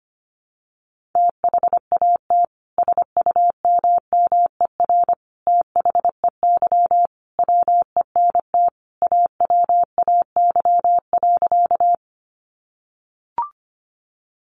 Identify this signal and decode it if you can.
Morse code, heard as a keyed tone, about 25 wpm, 700 Hz. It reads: THAT SUMMER THEY WENT AWAY.
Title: Sentences from Top 1100 Words 25wpm